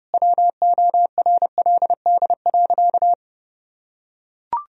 Morse code, heard as a keyed tone, about 28 wpm, 700 Hz. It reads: WORLD.
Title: Sentences from Top 1000 Words - Review Part 4 - 30wpm